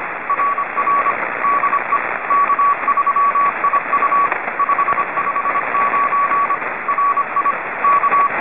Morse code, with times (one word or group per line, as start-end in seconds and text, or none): 0.3-1.1 RR
1.4-2.0 DE
2.3-4.3 K3IU
4.6-6.5 559
6.9-7.5 RI
7.8-8.3 K